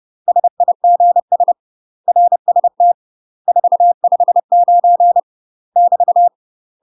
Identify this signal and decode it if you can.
Morse code, heard as a keyed tone, about 30 wpm, 700 Hz. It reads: SIGS RST 459 <BT>
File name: UK MASTER QSOs TRAINING / 30WPM / UK MASTER QSOs 0017